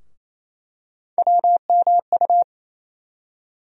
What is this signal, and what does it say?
Morse code, heard as a keyed tone, about 28 wpm, 700 Hz. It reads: WMU
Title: USA Call Sign Postfixes - 3 Letters 28wpm